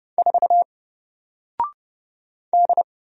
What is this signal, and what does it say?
Morse code, heard as a keyed tone, about 30 wpm, 700 Hz. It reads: 4 D
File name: Single-Letter-Number-30wpm